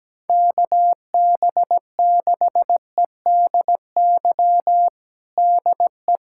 Morse code, heard as a keyed tone, about 17 wpm, 700 Hz. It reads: KB6EDY DE
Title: USA Callsign DE Callsign - 6 Characters - 17wpm